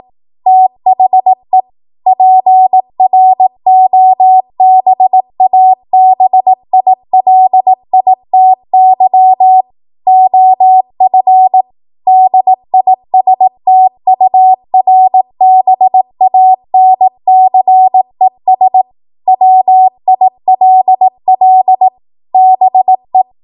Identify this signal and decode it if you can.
Morse code, heard as a keyed tone, about 18 wpm, 750 Hz. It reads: THE PROBABILITY OF DISTURBANCES WILL BE
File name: Morse / 18WPM